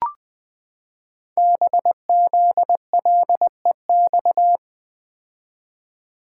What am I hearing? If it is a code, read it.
Morse code, heard as a keyed tone, about 20 wpm, 700 Hz. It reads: BZLEX
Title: Code Groups - 5 Letters - 20wpm